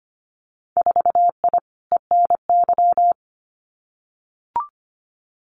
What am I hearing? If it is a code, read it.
Morse code, heard as a keyed tone, about 25 wpm, 700 Hz. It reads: 4I ENY
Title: Winter Field Day Exchanges 25wpm